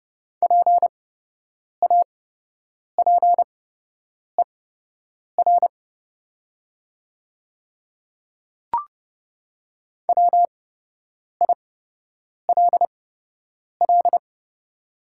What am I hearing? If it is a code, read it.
Morse code, heard as a keyed tone, about 30 wpm, spaced as for 8 wpm, 700 Hz. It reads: PAPER WILL
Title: 8x Character Spacing - Top 300 Words - 30wpm